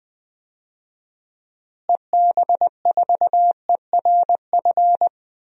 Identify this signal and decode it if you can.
Morse code, heard as a keyed tone, about 20 wpm, 700 Hz. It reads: EB4ERF
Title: DXCC-100 Call Signs - 6 Characters - Repeated 2x - 20wpm